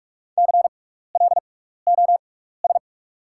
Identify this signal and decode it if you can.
Morse code, heard as a keyed tone, about 43 wpm, spaced as for 15 wpm, 700 Hz. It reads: CLOS
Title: 6x Character Spacing - Top 500 Words - 45wpm